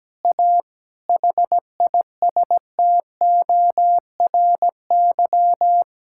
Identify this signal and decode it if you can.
Morse code, heard as a keyed tone, about 17 wpm, 700 Hz. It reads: A HISTORY
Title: Sentences from Top 1000 Words - Review Part 5 - 17wpm